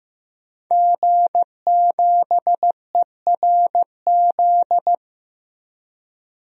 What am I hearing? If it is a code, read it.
Morse code, heard as a keyed tone, about 14 wpm, 700 Hz. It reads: G7ERZ
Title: DXCC-100 Call Signs - 5 Characters - Repeated 3x - 15wpm